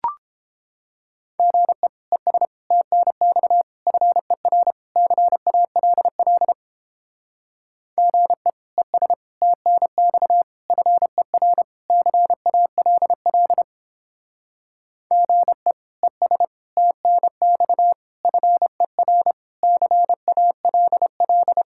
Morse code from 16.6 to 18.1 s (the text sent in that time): TNX